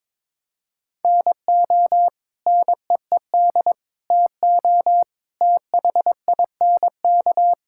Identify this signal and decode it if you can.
Morse code, heard as a keyed tone, about 22 wpm, 700 Hz. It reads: NO NEED TO THINK
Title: Sentences from Top 1600 Words 22wpm